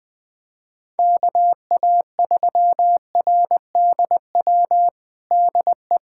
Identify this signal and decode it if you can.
Morse code, heard as a keyed tone, about 20 wpm, 700 Hz. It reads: KA3RDW DE
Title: USA Callsign DE Callsign 20wpm